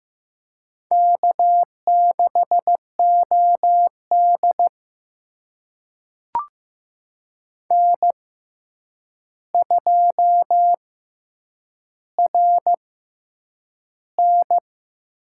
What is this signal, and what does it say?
Morse code, heard as a keyed tone, about 15 wpm, spaced as for 8 wpm, 700 Hz. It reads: K6OD N2RN
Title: USA Call Signs - 4 Characters - 6x Character Spacing - 15wpm